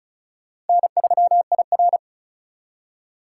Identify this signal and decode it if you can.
Morse code, heard as a keyed tone, about 45 wpm, 700 Hz. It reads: N3IR